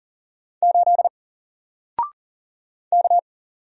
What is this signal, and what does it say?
Morse code, heard as a keyed tone, about 41 wpm, 700 Hz. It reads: 8 K
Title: Single Letter-Number Alt 40wpm